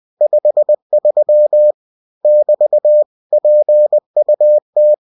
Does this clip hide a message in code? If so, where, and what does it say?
Yes: Morse code from 0.2 to 4.9 s, heard as a keyed tone, about 20 wpm, 600 Hz. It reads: 53 <BT> PUT